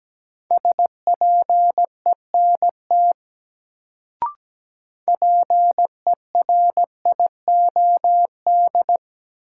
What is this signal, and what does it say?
Morse code, heard as a keyed tone, about 17 wpm, 700 Hz. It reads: SPENT PERIOD
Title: Sets of 5 Words - New Word Drill - 17wpm